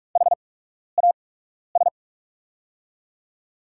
Morse code, heard as a keyed tone, about 43 wpm, spaced as for 12 wpm, 690 Hz. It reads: HAS
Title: 8x Character Spacing - Top 500 Words - 45wpm